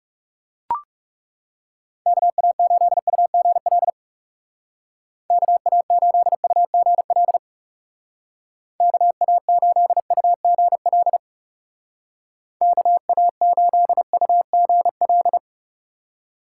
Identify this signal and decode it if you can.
Morse code, heard as a keyed tone, about 37 wpm, 700 Hz. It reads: KA8UGL KA8UGL KA8UGL KA8UGL
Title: Speed Racing - USA 2x3 Call Signs 30wpm